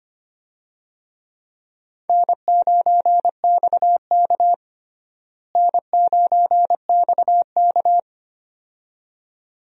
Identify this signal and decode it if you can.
Morse code, heard as a keyed tone, about 25 wpm, 700 Hz. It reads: N9XK N9XK
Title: USA Call Signs - 4 Characters - Repeated 2x - 25wpm